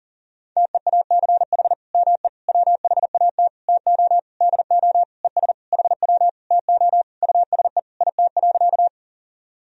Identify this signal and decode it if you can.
Morse code, heard as a keyed tone, about 40 wpm, 700 Hz. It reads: TEACH ME WHAT TO DO ES HW TO USE IT.
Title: Sentences from Top 500 Words and Abbreviations 40wpm